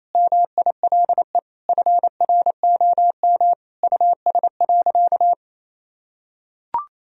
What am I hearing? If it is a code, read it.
Morse code, heard as a keyed tone, about 27 wpm, 700 Hz. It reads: MILE FROM US.
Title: Sentences from Top 1000 Words - Review Part 1 - 28wpm